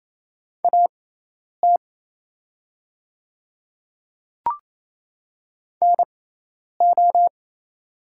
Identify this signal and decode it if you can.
Morse code, heard as a keyed tone, about 27 wpm, spaced as for 10 wpm, 700 Hz. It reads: AT NO
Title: 6x Character Spacing - Top 100 Words - 28wpm